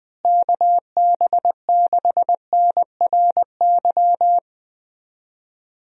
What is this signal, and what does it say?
Morse code, heard as a keyed tone, about 20 wpm, 700 Hz. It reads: KB6NRY
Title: USA Call Signs - 6 Characters - Repeated 3x - 20wpm